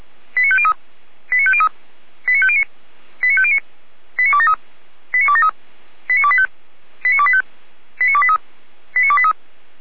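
Five-tone selective call (ZVEI), heard as quick runs of five tones, tones of about 70 ms. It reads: 89503 89503 89509 89509 89273 89273 89285 89285 89283 89283
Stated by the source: ZVEI five-tone selective call, code 89108, received with an SDR, archived by Feuerwehr Orlamünde.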